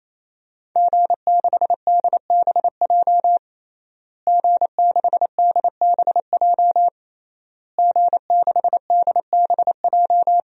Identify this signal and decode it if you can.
Morse code, heard as a keyed tone, about 27 wpm, 700 Hz. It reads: G6DBJ G6DBJ G6DBJ
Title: DXCC-100 Call Signs - 5 Characters - Repeated 3x - 28wpm